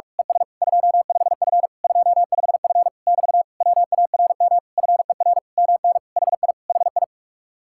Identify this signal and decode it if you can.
Morse code, heard as a keyed tone, about 45 wpm, 700 Hz. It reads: ES 25F 25F <BT> WARM FER MN HI HI
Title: Speed-Racing Common QSO Phrases Encore - 45wpm